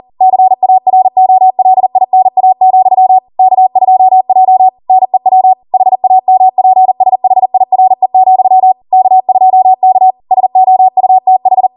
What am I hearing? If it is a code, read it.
Morse code, heard as a keyed tone, about 40 wpm, 750 Hz. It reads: CAROLINA, K2J NEW HAMPSHIRE, K2K SOUTH